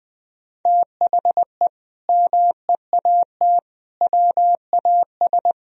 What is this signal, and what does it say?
Morse code, heard as a keyed tone, about 20 wpm, 700 Hz. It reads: THE MEAT WAS
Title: Sentences from Top 1000 Words - Review Part 3 - 20wpm